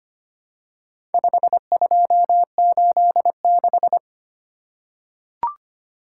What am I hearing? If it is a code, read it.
Morse code, heard as a keyed tone, about 25 wpm, 700 Hz. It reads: 5286
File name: practice / 25wpm / Four Letter-Number Combinations - 25wpm